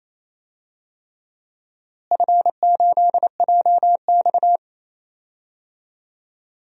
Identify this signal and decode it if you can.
Morse code, heard as a keyed tone, about 28 wpm, 700 Hz. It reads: F8JX
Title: DXCC-100 Call Signs - 4 Characters - Repeated 3x - 28wpm